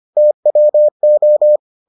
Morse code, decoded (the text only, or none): TWO